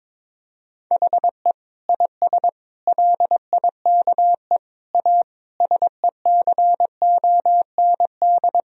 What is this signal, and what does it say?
Morse code, heard as a keyed tone, about 22 wpm, 700 Hz. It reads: HE IS LIKE A SECOND
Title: Sentences from Top 1000 Words - Review Part 1 - 22wpm